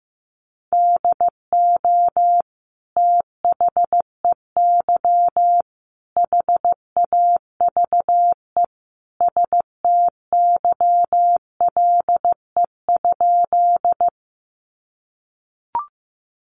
Morse code, heard as a keyed tone, about 15 wpm, 700 Hz. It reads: DO THEY HAVE STYLE?
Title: Sentences from Top 800 Words 15wpm